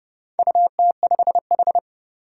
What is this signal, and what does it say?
Morse code, heard as a keyed tone, about 30 wpm, 700 Hz. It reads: UT5H
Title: Four Challenging Letter-Number Combinations 30wpm